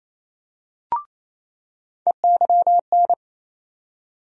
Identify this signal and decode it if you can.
Morse code, heard as a keyed tone, about 28 wpm, 700 Hz. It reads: EYN
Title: USA Call Sign Postfixes - 3 Letters 28wpm